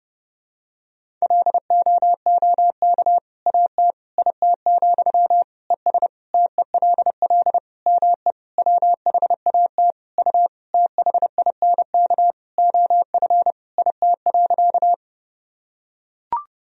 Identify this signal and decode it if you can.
Morse code, heard as a keyed tone, about 32 wpm, 700 Hz. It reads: LOOK AT IT, ES TELL ME WHAT U THINK OF IT.
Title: Sentences from Top 200 Words and Abbreviations 30wpm